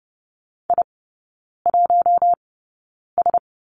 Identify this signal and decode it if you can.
Morse code, heard as a keyed tone, about 30 wpm, spaced as for 9 wpm, 700 Hz. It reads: I1S
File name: course / 30:10wpm / 087 Lesson 15e - Number 5 - Three Characters 30wpm